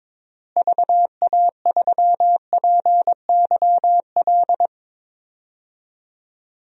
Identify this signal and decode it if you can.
Morse code, heard as a keyed tone, about 22 wpm, 700 Hz. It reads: VA3PYL
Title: DXCC-100 Call Signs - 6 Characters - Repeated 3x - 22wpm